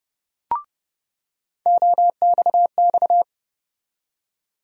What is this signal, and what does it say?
Morse code, heard as a keyed tone, about 31 wpm, 700 Hz. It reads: OXX
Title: USA Call Sign Postfixes - 3 Letters 30wpm